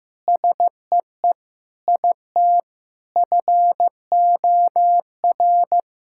Morse code, heard as a keyed tone, about 15 wpm, 700 Hz. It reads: SEE IT FOR
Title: Sentences from Top 1600 Words 15wpm